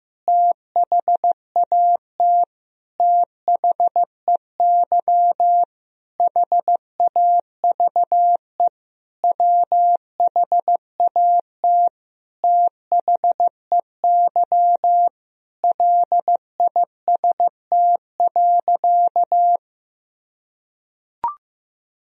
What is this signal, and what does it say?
Morse code, heard as a keyed tone, about 15 wpm, 700 Hz. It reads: THAT THEY HAVE WHAT THEY LIST.
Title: Sentences from Top 1000 Words - Review Part 2 - 15wpm